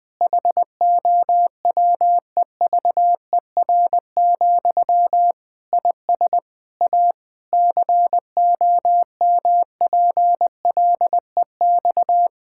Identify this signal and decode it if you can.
Morse code, heard as a keyed tone, about 20 wpm, 700 Hz. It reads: HOWEVER, IS A COMPLEX
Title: Sentences from Top 1000 Words - Review Part 6 - 20wpm